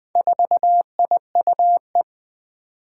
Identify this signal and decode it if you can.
Morse code, heard as a keyed tone, about 20 wpm, 700 Hz. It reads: 4IUE
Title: Four Challenging Letter-Number Combinations 20wpm